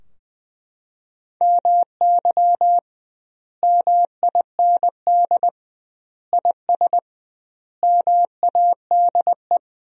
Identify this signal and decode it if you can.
Morse code, heard as a keyed tone, about 21 wpm, 700 Hz. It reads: MY MIND IS MADE
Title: 2x Word-Spacing - Sentences from Top 500 Words - 20wpm